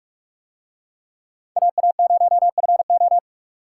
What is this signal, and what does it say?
Morse code, heard as a keyed tone, about 45 wpm, 700 Hz. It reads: AA0FO